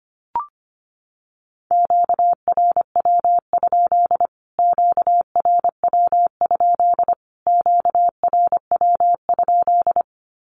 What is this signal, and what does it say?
Morse code, heard as a keyed tone, about 25 wpm, 700 Hz. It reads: QRW? QRW? QRW?
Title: Q-Codes Repeated 3x 25wpm